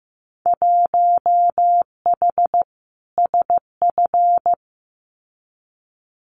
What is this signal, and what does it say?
Morse code, heard as a keyed tone, about 15 wpm, 700 Hz. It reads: 1H SF
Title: Winter Field Day Exchanges 15wpm